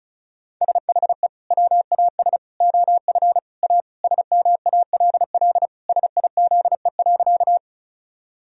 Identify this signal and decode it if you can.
Morse code, heard as a keyed tone, about 36 wpm, 700 Hz. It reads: SHE WAS OF A SMALL SIZE.
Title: Sentences from Top 1000 Words - Review Part 2 - 35wpm